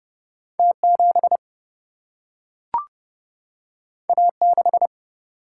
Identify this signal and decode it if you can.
Morse code, heard as a keyed tone, about 30 wpm, 700 Hz. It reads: T 7 A 6
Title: Single-Letter Single-Number 30wpm